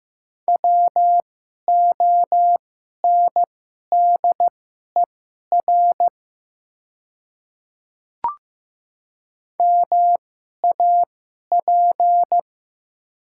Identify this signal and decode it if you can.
Morse code, heard as a keyed tone, about 15 wpm, spaced as for 11 wpm, 700 Hz. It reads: WONDER MAP
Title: 2x Character Spacing - Top 500 Words - 15wpm